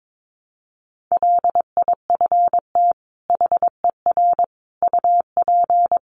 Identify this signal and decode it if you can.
Morse code, heard as a keyed tone, about 22 wpm, 700 Hz. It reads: LIFT HER UP
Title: Sentences from Top 1300 Words 22wpm